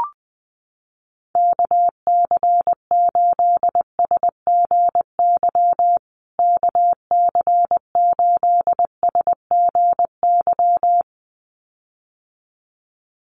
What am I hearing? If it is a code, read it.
Morse code, heard as a keyed tone, about 20 wpm, 700 Hz. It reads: KC8SGY KC8SGY
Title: USA Call Signs Repeated 2x 20wpm